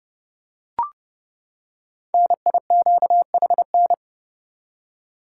Morse code, heard as a keyed tone, about 30 wpm, 700 Hz. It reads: NIQHN
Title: Code Groups - 5 Letter:Numbers - 30wpm